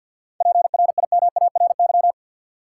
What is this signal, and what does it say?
Morse code, heard as a keyed tone, about 53 wpm, 700 Hz. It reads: PRIMARY
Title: Seven Letter Words - 50wpm